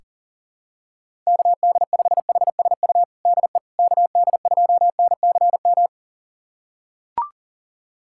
Morse code, heard as a keyed tone, about 40 wpm, 700 Hz. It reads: KD5HSU DE KD2NCM
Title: USA Callsign DE Callsign - 6 Characters - 40wpm